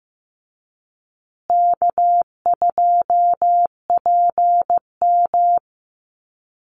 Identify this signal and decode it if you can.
Morse code, heard as a keyed tone, about 15 wpm, 700 Hz. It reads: K2PM
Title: USA Call Signs Repeated 3x 15wpm